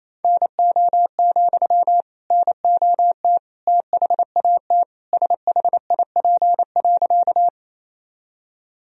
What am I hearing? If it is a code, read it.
Morse code, heard as a keyed tone, about 28 wpm, 700 Hz. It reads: NO, NOT THAT SHIP.
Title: Sentences from Top 1000 Words - Review Part 2 - 28wpm